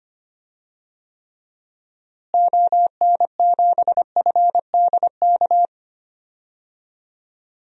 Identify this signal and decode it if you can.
Morse code, heard as a keyed tone, about 25 wpm, 700 Hz. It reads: ON7FDK
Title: DXCC-100 Call Signs - 6 Characters - Repeated 3x - 25wpm